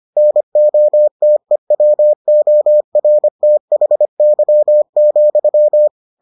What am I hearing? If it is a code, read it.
Morse code, heard as a keyed tone, about 25 wpm, 600 Hz. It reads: NOTEWORTHY,